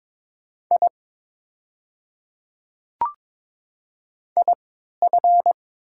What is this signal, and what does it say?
Morse code, heard as a keyed tone, about 22 wpm, spaced as for 13 wpm, 690 Hz. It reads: I IF
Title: 3x Character Spacing - Top 200 Words - 22wpm